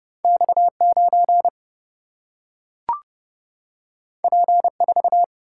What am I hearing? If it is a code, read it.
Morse code, heard as a keyed tone, about 30 wpm, 700 Hz. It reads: X9 P4
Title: Single-Letter Single-Number 30wpm